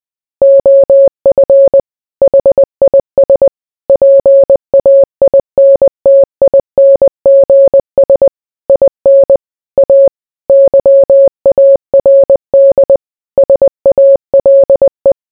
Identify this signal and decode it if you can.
Morse code, heard as a keyed tone, about 20 wpm, 550 Hz. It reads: OF HIS PAINTINGS IN A YARD SALE